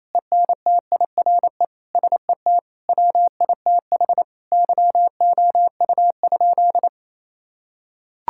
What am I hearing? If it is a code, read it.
Morse code, heard as a keyed tone, about 28 wpm, 700 Hz. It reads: ENTIRE SET WITH YOU?
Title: Sentences from Top 1000 Words - Review Part 5 - 28wpm